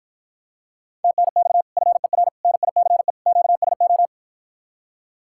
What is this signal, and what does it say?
Morse code, heard as a keyed tone, about 54 wpm, 700 Hz. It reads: TNX FER NICE QSO